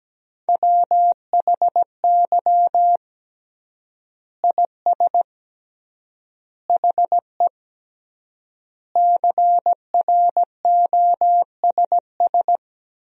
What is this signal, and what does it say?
Morse code, heard as a keyed tone, about 17 wpm, 700 Hz. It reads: WHY IS HE CROSS
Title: 3x Word-Spacing - Sentences from Top 300 Words - 17wpm